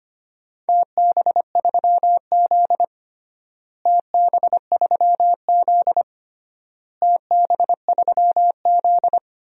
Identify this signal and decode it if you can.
Morse code, heard as a keyed tone, about 25 wpm, 700 Hz. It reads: TB3Z TB3Z TB3Z
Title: DXCC-100 Call Signs - 4 Characters - Repeated 3x - 25wpm